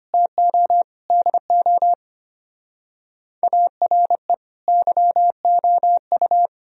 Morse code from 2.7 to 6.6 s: ARE YOU

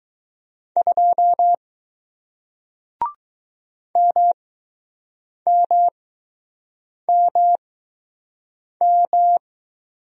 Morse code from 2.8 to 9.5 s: MMMM

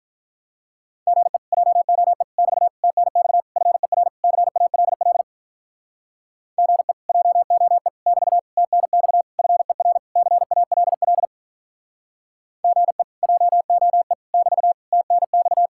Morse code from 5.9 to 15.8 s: GE JOE <BT> TNX FER CALL GE JOE <BT> TNX

